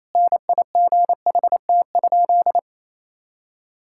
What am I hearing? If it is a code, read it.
Morse code, heard as a keyed tone, about 28 wpm, 700 Hz. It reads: NIGHT?
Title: Sentences from Top 1000 Words - Review Part 5 - 28wpm